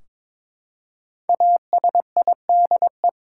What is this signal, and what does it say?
Morse code, heard as a keyed tone, about 21 wpm, 700 Hz. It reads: ASIDE
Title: Five Letter Words - 22wpm